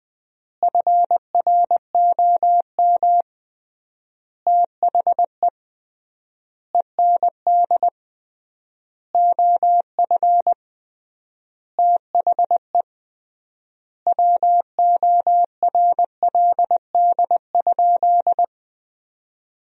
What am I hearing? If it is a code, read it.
Morse code, heard as a keyed tone, about 20 wpm, 700 Hz. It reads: FROM THE END OF THE WORLD?
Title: 3x Word-Spacing - Sentences from Top 200 Words - 20wpm